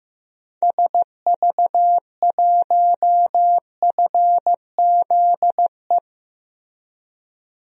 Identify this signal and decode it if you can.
Morse code, heard as a keyed tone, about 15 wpm, 700 Hz. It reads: SV1FZE
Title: DXCC-100 Call Signs - 6 Characters - Repeated 3x - 15wpm